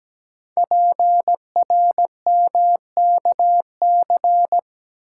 Morse code, heard as a keyed tone, about 17 wpm, 700 Hz. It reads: PRMKC
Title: Code Groups - 5 Letters - 17wpm